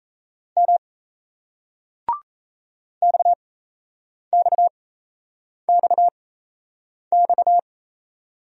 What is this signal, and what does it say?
Morse code, heard as a keyed tone, about 38 wpm, spaced as for 8 wpm, 700 Hz. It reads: M XXXX